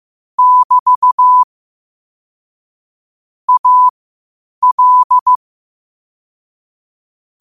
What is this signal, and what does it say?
Morse code, heard as a keyed tone, about 15 wpm, spaced as for 8 wpm, 1000 Hz. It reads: <BT> AL
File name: qotdfiles / 8wpm